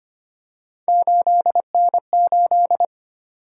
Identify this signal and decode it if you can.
Morse code, heard as a keyed tone, about 25 wpm, 700 Hz. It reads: 8N8
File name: Three Letter-Number Combinations - 25wpm